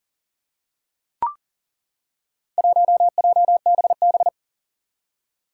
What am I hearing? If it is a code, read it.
Morse code, heard as a keyed tone, about 37 wpm, 700 Hz. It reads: 1JBB
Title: Four Challenging Letter-Number Combinations 40wpm